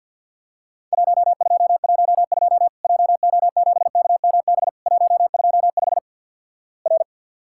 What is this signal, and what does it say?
Morse code, heard as a keyed tone, about 48 wpm, 700 Hz. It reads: 1212 JO7KMB 125 R
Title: Worked All Europe DX Contest - QTC Exchange with Repeats - 50wpm